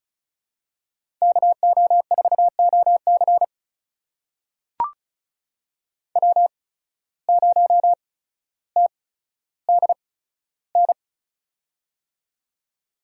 Morse code, heard as a keyed tone, about 36 wpm, spaced as for 15 wpm, 700 Hz. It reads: KO4OC W0TDN
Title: USA Call Signs - 5 Characters - 8x Character Spacing - 35wpm